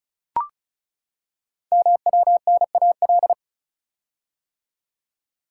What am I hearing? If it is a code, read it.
Morse code, heard as a keyed tone, about 36 wpm, 700 Hz. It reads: MWNAL
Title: Code Groups - 5 Letters - 35wpm